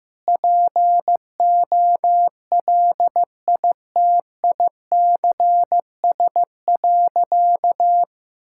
Morse code, heard as a keyed tone, about 15 wpm, 700 Hz. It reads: POLITICS.
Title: Sentences from Top 1000 Words - Review Part 6 - 15wpm